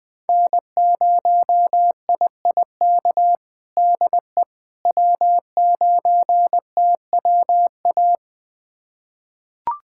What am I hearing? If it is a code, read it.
Morse code, heard as a keyed tone, about 20 wpm, 700 Hz. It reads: N0IIK DE W9TWA
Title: USA Callsign DE Callsign - 5 Characters - 20wpm